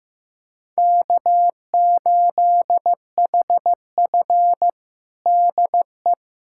Morse code, heard as a keyed tone, about 15 wpm, 700 Hz. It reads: K8HF DE
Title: USA Callsign DE Callsign - 4 Characters - 15wpm